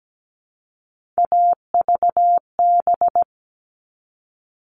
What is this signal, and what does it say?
Morse code, heard as a keyed tone, about 17 wpm, 700 Hz. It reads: AVB
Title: All Cut Numbers - 3-Digit 17wpm